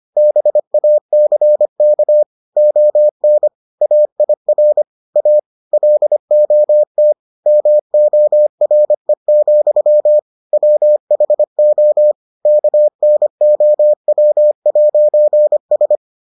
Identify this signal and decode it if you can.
Morse code, heard as a keyed tone, about 25 wpm, 600 Hz. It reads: BACK ON AIR A LOT MORE, WHO KNOW'S